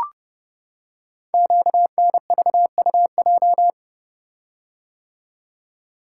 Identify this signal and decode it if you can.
Morse code, heard as a keyed tone, about 29 wpm, 700 Hz. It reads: QNVUJ
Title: Code Groups - 5 Letters - 30wpm